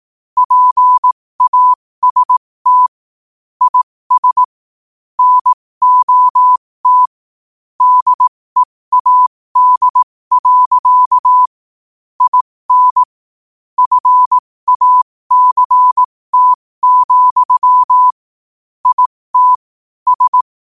Morse code, heard as a keyed tone, about 18 wpm, spaced as for 15 wpm, 1000 Hz. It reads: PAST IS NOT DEAD. IN FACT, IT S
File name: qotdfiles / 15wpm